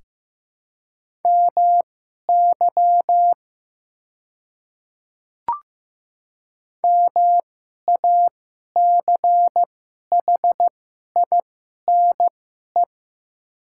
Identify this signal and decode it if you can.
Morse code, heard as a keyed tone, about 15 wpm, spaced as for 11 wpm, 700 Hz. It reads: MY MACHINE
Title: 2x Character Spacing - Top 500 Words - 15wpm